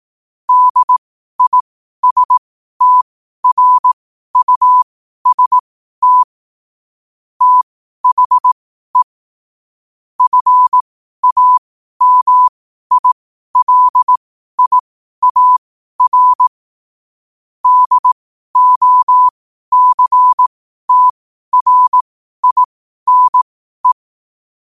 Morse code, heard as a keyed tone, about 18 wpm, spaced as for 12 wpm, 1000 Hz. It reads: DISTRUST THE FAMILIAR DOCTRINE
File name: qotdfiles / 12wpm